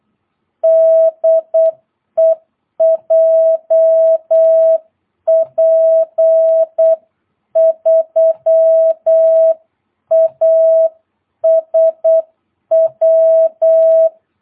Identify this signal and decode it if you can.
Morse code, heard as a keyed tone, about 8 wpm, 650 Hz. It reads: DE JP3ASW
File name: jh8cbh / 2022cw